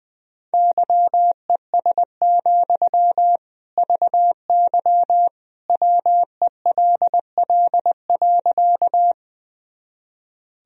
Morse code, heard as a keyed tone, about 20 wpm, 700 Hz. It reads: YES, VY WELL.
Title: Sentences from Top 500 Words and Abbreviations 20wpm